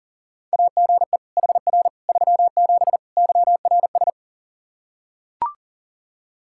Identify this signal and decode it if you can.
Morse code, heard as a keyed tone, about 40 wpm, 700 Hz. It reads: AGE HR 37 YRS
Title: Practice QSO - Exchange 3 - More Info - 40wpm